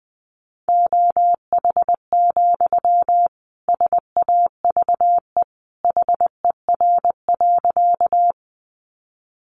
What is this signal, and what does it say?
Morse code, heard as a keyed tone, about 20 wpm, 700 Hz. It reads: OH, SAVE HER.
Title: Sentences from Top 1100 Words 20wpm